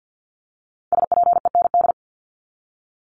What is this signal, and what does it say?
Morse code, heard as a keyed tone, about 50 wpm, 700 Hz. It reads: SPEND
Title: Top 501-750 Words 50wpm